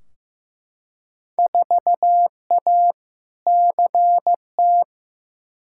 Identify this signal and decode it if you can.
Morse code, heard as a keyed tone, about 15 wpm, 700 Hz. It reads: 4A CT
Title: Field Day Exchanges Encore - 15wpm